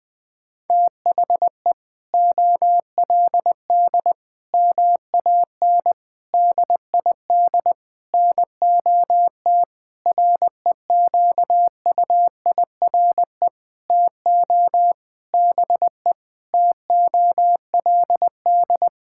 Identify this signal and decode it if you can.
Morse code, heard as a keyed tone, about 20 wpm, 700 Hz. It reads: THE OLD MAN DID NOT REQUIRE TO BE TOLD
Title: Sentences from Top 1600 Words 20wpm